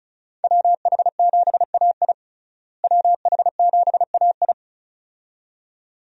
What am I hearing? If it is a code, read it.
Morse code, heard as a keyed tone, about 36 wpm, 700 Hz. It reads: WH7AI WH7AI
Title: USA Call Signs - 5 Characters - Repeated 2x - 35wpm